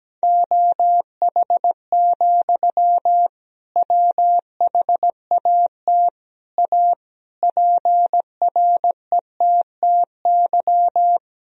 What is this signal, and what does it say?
Morse code, heard as a keyed tone, about 17 wpm, 700 Hz. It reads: OH, WHAT A PRETTY
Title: Sentences from Top 1000 Words - Review Part 2 - 17wpm